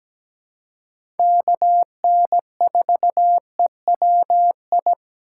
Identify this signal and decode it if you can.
Morse code, heard as a keyed tone, about 17 wpm, 700 Hz. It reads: KN4EWI
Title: K1USN SST QSO Practice - 17wpm